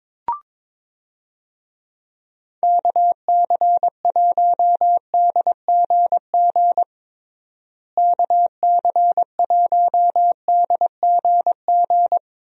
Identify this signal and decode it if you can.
Morse code, heard as a keyed tone, about 22 wpm, 700 Hz. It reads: KC1DGG KC1DGG
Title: USA Call Signs - 6 Characters - Repeated 2x - 22wpm